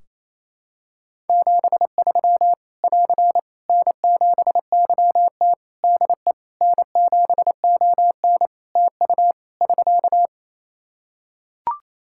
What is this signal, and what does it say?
Morse code, heard as a keyed tone, about 28 wpm, 700 Hz. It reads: 73 <AR> N7YT DE N7ON TU <SK>
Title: Practice QSO - Exchange 4 - Ending - 28wpm